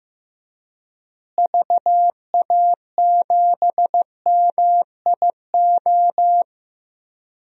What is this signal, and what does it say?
Morse code, heard as a keyed tone, about 15 wpm, 700 Hz. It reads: VA7MIO
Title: DXCC-100 Call Signs - 6 Characters - Repeated 3x - 15wpm